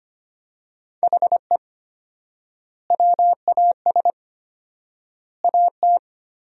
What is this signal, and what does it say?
Morse code, heard as a keyed tone, about 25 wpm, 700 Hz. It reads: HE WAS AT
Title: 4x Word-Spacing - Sentences from Top 200 Words - 25wpm